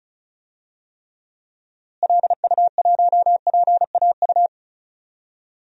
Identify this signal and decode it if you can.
Morse code, heard as a keyed tone, about 36 wpm, 700 Hz. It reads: LU1PAU
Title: DXCC-100 Call Signs - 6 Characters - Repeated 2x - 35wpm